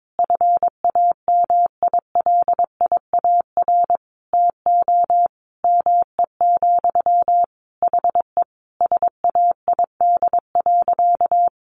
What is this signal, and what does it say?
Morse code, heard as a keyed tone, about 22 wpm, 700 Hz. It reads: FAMILIAR TO ME, HE SAID.
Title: Sentences from Top 1000 Words 22wpm